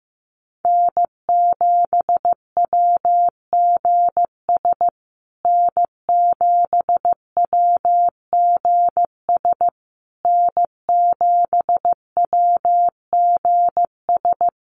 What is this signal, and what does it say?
Morse code, heard as a keyed tone, about 15 wpm, 700 Hz. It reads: N7WGS N7WGS N7WGS
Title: USA Call Signs Repeated 3x 15wpm